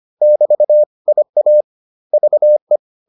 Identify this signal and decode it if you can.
Morse code, heard as a keyed tone, about 25 wpm, 600 Hz. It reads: <BT> IA VE